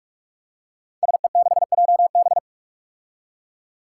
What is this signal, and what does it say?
Morse code, heard as a keyed tone, about 46 wpm, 700 Hz. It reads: SE6JB
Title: Five Challenging Letter-Number Combinations 45wpm